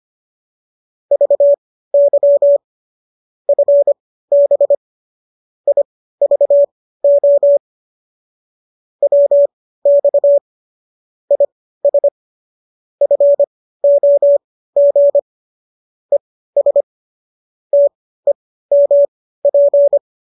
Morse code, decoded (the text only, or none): VY FB IVO WX IS FOG ES TEMP